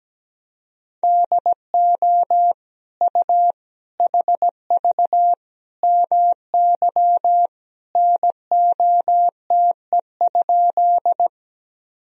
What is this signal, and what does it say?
Morse code, heard as a keyed tone, about 17 wpm, 700 Hz. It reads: DO U HV MY NOTE?
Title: Sentences from Top 500 Words and Abbreviations 17wpm